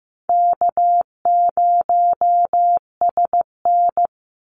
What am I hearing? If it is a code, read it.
Morse code, heard as a keyed tone, about 15 wpm, 700 Hz. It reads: K0SN
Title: Call Signs Repeated 3x - 15wpm